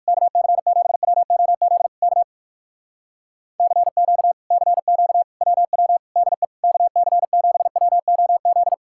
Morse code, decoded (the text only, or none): KC7WOZ K CQ CQ WW DE KC7WOZ